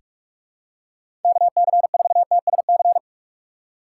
Morse code, heard as a keyed tone, about 46 wpm, 700 Hz. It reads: KC4TSC
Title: USA Call Signs - 6 Characters - 2x Character Spacing - 45wpm